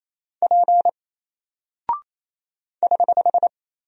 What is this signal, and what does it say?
Morse code, heard as a keyed tone, about 28 wpm, 700 Hz. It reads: P <HH>
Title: Single Character (Letter:Number:Punctuation:Prosign) 28wpm